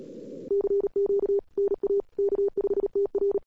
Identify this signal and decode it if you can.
Morse code, heard as a keyed tone, about 38 wpm, 400 Hz. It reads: CQ NA K5TR
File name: cw-sprint-k5tr-op-n5rz